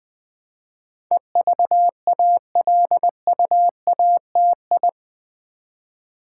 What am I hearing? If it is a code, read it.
Morse code, heard as a keyed tone, about 20 wpm, 700 Hz. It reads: EVALUATI
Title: Build A Word - 4 Letters At A Time - 10-Letter Words - 20wpm